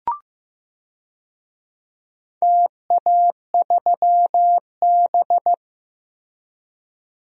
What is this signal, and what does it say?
Morse code, heard as a keyed tone, about 15 wpm, 700 Hz. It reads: TA3B
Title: DXCC-100 Call Signs - 4 Characters - Repeated 3x - 15wpm